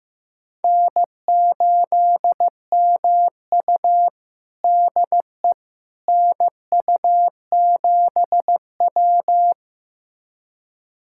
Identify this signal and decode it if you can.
Morse code, heard as a keyed tone, about 15 wpm, 700 Hz. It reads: N8MU DE NU7W
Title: USA Callsign DE Callsign - 4 Characters - 15wpm